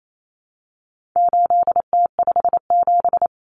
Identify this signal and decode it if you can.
Morse code, heard as a keyed tone, about 28 wpm, 700 Hz. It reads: 8T57
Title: Common Cut Numbers - 4-Digit 28wpm